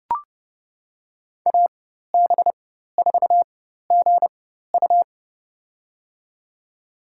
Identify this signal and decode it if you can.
Morse code, heard as a keyed tone, about 30 wpm, spaced as for 14 wpm, 700 Hz. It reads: AB4GU
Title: USA Call Signs - 5 Characters - 4x Character Spacing - 30wpm